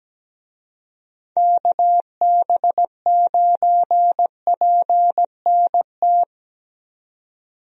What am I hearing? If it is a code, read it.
Morse code, heard as a keyed tone, about 17 wpm, 700 Hz. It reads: KB9PNT
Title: USA Call Signs - 6 Characters - 17wpm